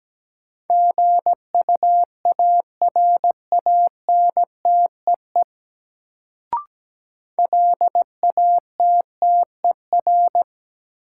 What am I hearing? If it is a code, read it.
Morse code, heard as a keyed tone, about 17 wpm, 700 Hz. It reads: GUARANTEE LATTER
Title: Sentences from Top 1600 Words - New Word Drill Alt - 17wpm